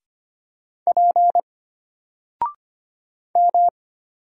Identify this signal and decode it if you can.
Morse code, heard as a keyed tone, about 25 wpm, 700 Hz. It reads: P M
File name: Single-Letter-Number-25wpm